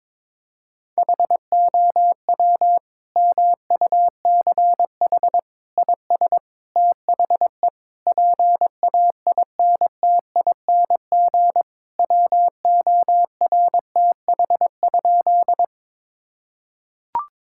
Morse code, heard as a keyed tone, about 22 wpm, 700 Hz. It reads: HOW MUCH IS THE PAINTING WORTH?
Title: Sentences from Top 1000 Words - Review Part 6 - 22wpm